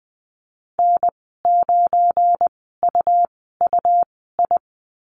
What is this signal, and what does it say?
Morse code, heard as a keyed tone, about 20 wpm, spaced as for 15 wpm, 700 Hz. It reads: N9UUI
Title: 186 Lesson 30g - Letter b - Call Signs 20wpm